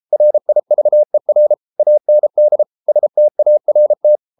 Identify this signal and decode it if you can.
Morse code, heard as a keyed tone, about 33 wpm, 600 Hz. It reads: RIVER AND START